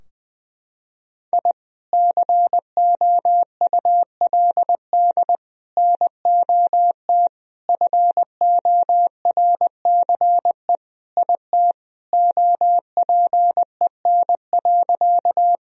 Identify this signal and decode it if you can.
Morse code, heard as a keyed tone, about 20 wpm, 700 Hz. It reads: I COULD NOT FORCE IT OPEN.
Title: Sentences from Top 1000 Words - Review Part 3 - 20wpm